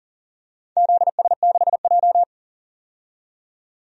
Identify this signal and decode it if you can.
Morse code, heard as a keyed tone, about 40 wpm, 700 Hz. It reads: ZS6J
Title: DXCC-100 Call Signs - 4 Characters - 40wpm